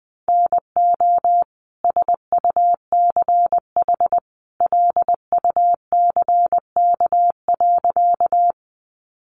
Morse code, heard as a keyed tone, about 20 wpm, 700 Hz. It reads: NO SUCH LUCK.
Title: Sentences from Top 1500 Words 20wpm